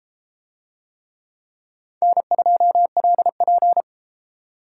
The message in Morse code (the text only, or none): N2LP